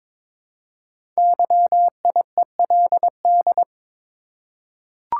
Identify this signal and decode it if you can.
Morse code, heard as a keyed tone, about 22 wpm, 700 Hz. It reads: YIELD
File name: practice / 22wpm / 5 Letter Words - First 3 Letters Given - 22wpm